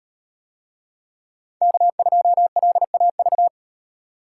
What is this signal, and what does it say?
Morse code, heard as a keyed tone, about 37 wpm, 700 Hz. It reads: K2LAV